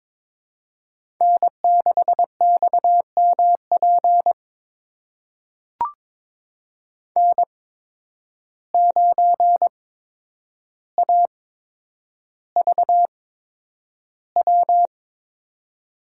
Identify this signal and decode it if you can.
Morse code, heard as a keyed tone, about 22 wpm, spaced as for 9 wpm, 700 Hz. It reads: N6XMP N9AVW
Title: USA Call Signs - 5 Characters - 8x Character Spacing - 22wpm